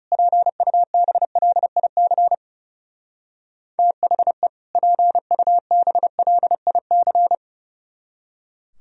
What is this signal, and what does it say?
Morse code, heard as a keyed tone, about 37 wpm, 700 Hz. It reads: PUBLIC THE PUBLIC
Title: Speed-Racing - Sets of 2 Words 30wpm